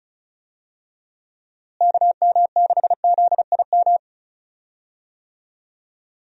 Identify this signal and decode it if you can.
Morse code, heard as a keyed tone, about 36 wpm, 700 Hz. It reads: KM6ZIM
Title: USA Call Signs - 6 Characters - Repeated 2x - 35wpm